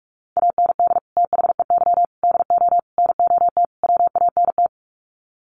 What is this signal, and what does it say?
Morse code, heard as a keyed tone, about 44 wpm, 700 Hz. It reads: AND THEY DO NOT WANT
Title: Sets of 5 Words Encore2 45wpm